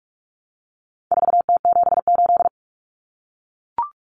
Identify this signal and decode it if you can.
Morse code, heard as a keyed tone, about 44 wpm, 700 Hz. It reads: 4T78
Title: Common Cut Numbers - 4-Digit 45wpm